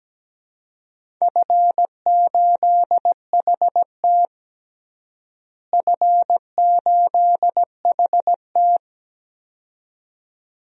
Morse code, heard as a keyed tone, about 17 wpm, 700 Hz. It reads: F8HT F8HT
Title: DXCC-100 Call Signs - 4 Characters - Repeated 2x - 17wpm